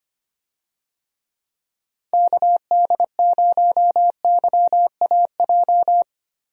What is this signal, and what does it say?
Morse code, heard as a keyed tone, about 25 wpm, 700 Hz. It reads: KD0YAJ
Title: USA Call Signs - 6 Characters - Repeated 3x - 25wpm